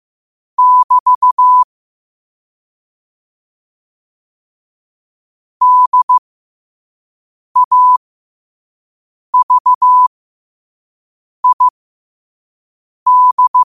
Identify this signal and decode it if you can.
Morse code, heard as a keyed tone, about 15 wpm, spaced as for 5 wpm, 1000 Hz. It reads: <BT> DAVID
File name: qotdfiles / 5wpm